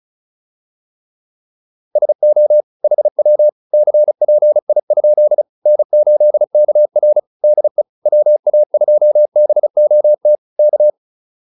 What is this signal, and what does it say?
Morse code, heard as a keyed tone, about 37 wpm, 600 Hz. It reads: SO HW CPI? N8KR DE WA2BOT K